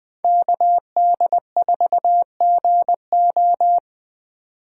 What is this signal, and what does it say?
Morse code, heard as a keyed tone, about 22 wpm, 700 Hz. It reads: KD4GO